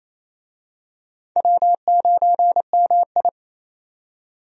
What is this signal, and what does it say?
Morse code, heard as a keyed tone, about 27 wpm, 700 Hz. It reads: W9MI
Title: USA Call Signs - 4 Characters - 28wpm